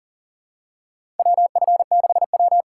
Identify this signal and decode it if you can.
Morse code, heard as a keyed tone, about 40 wpm, 700 Hz. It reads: WF6W
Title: USA Call Signs - 4 Characters - 6x Character Spacing - 40wpm